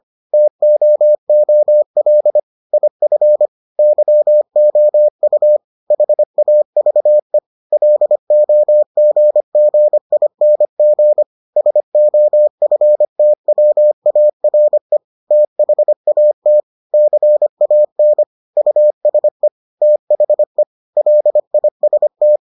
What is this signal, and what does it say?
Morse code, heard as a keyed tone, about 25 wpm, 600 Hz. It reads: TOOL IF YOU HAVE LOGGING SOFTWARE THAT CAN USE THE LIST